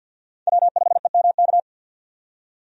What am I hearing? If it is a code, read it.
Morse code, heard as a keyed tone, about 50 wpm, 700 Hz. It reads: W5EMK
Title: USA Call Signs - 5 Characters - 50wpm